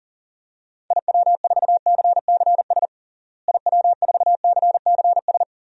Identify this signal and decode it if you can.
Morse code, heard as a keyed tone, about 40 wpm, 700 Hz. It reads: IW4CCS IW4CCS
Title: DXCC-100 Call Signs - 6 Characters - Repeated 3x - 40wpm